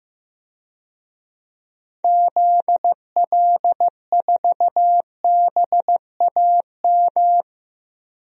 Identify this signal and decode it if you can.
Morse code, heard as a keyed tone, about 15 wpm, 700 Hz. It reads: ZL4BAM
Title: DXCC-100 Call Signs - 6 Characters - 15wpm